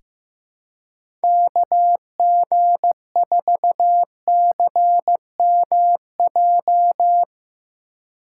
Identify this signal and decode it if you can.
Morse code, heard as a keyed tone, about 15 wpm, 700 Hz. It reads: KG4CMJ
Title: USA Call Signs - 6 Characters - 2x Character Spacing - 15wpm